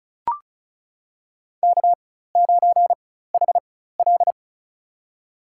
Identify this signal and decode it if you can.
Morse code, heard as a keyed tone, about 36 wpm, spaced as for 16 wpm, 700 Hz. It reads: K9HL
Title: USA Call Signs - 4 Characters - 4x Character Spacing - 35wpm